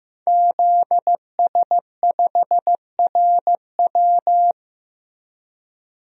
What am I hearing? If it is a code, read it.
Morse code, heard as a keyed tone, about 15 wpm, 700 Hz. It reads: ZS5RW
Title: DXCC-100 Call Signs - 5 Characters - Repeated 3x - 15wpm